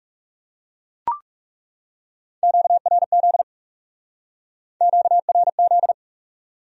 Morse code, heard as a keyed tone, about 39 wpm, 700 Hz. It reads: QRZ QRZ